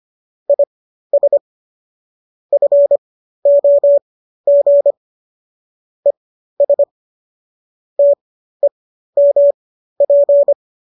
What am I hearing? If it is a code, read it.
Morse code, heard as a keyed tone, about 25 wpm, spaced as for 13 wpm, 570 Hz. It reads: IS FOG ES TEMP